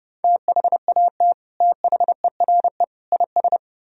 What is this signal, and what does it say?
Morse code, heard as a keyed tone, about 30 wpm, 700 Hz. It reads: THAT THERE IS